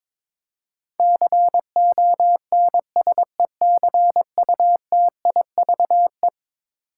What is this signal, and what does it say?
Morse code, heard as a keyed tone, about 22 wpm, 700 Hz. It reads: CONSECUTIVE
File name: practice / 22wpm / Eleven Letter Words - 22wpm